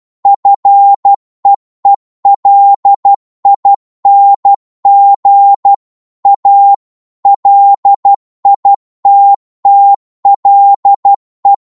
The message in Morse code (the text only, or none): FEELING A LITTLE